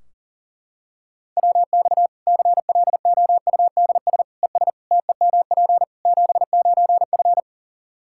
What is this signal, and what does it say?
Morse code, heard as a keyed tone, about 40 wpm, 700 Hz. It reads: WX CLOUDS ES TEMP 79F
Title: Practice QSO - Exchange 2 - Info - 40wpm